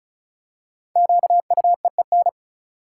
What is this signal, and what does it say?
Morse code, heard as a keyed tone, about 34 wpm, 700 Hz. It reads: QUEEN
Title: Five Letter Words - 35wpm